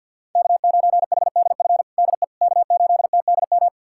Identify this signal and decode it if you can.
Morse code, heard as a keyed tone, about 52 wpm, 690 Hz. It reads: K9HDF DE K8TDM
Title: USA Callsign DE Callsign - 5 Characters - 50wpm